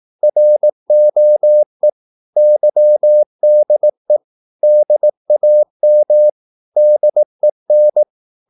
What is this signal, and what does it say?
Morse code, heard as a keyed tone, about 18 wpm, 600 Hz. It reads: ROE YDE DAM DEN
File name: Korte ord DK 18wpm